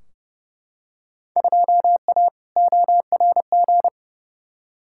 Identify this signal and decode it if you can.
Morse code, heard as a keyed tone, about 29 wpm, 700 Hz. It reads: 2A ORG
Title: Field Day Exchanges Encore - 30wpm